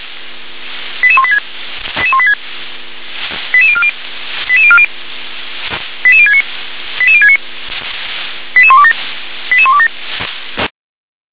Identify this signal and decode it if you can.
Five-tone selective call (ZVEI), heard as quick runs of five tones, tones of about 70 ms. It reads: 88186 88186 88040 88040 88069 88069 88127 88127
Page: SDR/ZVEI